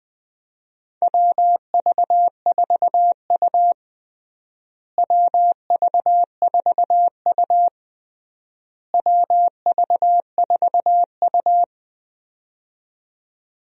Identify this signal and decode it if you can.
Morse code, heard as a keyed tone, about 20 wpm, 700 Hz. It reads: WV4U WV4U WV4U
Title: USA Call Signs - 4 Characters - Repeated 3x - 20wpm